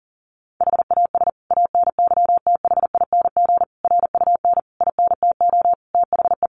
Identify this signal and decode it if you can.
Morse code, heard as a keyed tone, about 40 wpm, 700 Hz. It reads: HAS ANYTHING RUN INTO THE
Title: Sentences from Top 1400 Words 40wpm